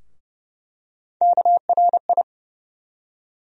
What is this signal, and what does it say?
Morse code, heard as a keyed tone, about 30 wpm, 700 Hz. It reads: KRI
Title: USA Call Sign Postfixes - 3 Letters 30wpm